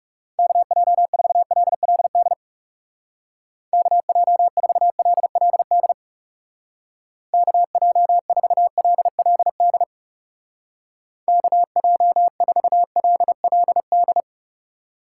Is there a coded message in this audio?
Yes: Morse code from 0.4 to 14.2 s, heard as a keyed tone, about 39 wpm, 700 Hz. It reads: KJ4LLD KJ4LLD KJ4LLD KJ4LLD